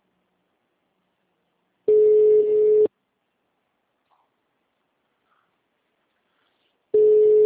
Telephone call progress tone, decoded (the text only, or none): ringback tone